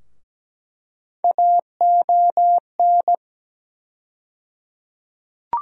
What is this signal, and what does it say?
Morse code, heard as a keyed tone, about 17 wpm, 700 Hz. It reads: AON
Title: USA Call Sign Postfixes - 3 Letters 17wpm